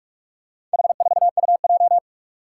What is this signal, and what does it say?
Morse code, heard as a keyed tone, about 42 wpm, 700 Hz. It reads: H4UJ
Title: Four Challenging Letter-Number Combinations 45wpm